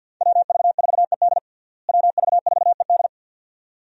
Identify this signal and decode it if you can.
Morse code, heard as a keyed tone, about 53 wpm, 700 Hz. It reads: WV4ED WV4ED
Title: USA Call Signs - 5 Characters - Repeated 2x - 50wpm